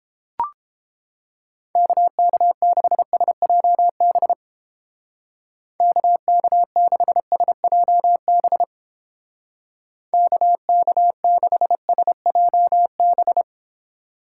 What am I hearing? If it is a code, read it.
Morse code, heard as a keyed tone, about 29 wpm, 700 Hz. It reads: KK6SJB KK6SJB KK6SJB